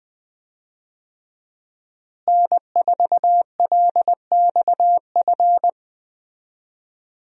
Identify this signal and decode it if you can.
Morse code, heard as a keyed tone, about 20 wpm, 700 Hz. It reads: N4LXF
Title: USA Call Signs - 5 Characters - Repeated 2x - 20wpm